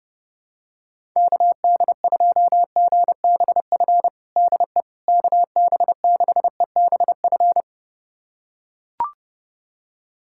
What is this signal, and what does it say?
Morse code, heard as a keyed tone, about 30 wpm, 700 Hz. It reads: KD2GBF DE KB6EBF
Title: USA Callsign DE Callsign 30wpm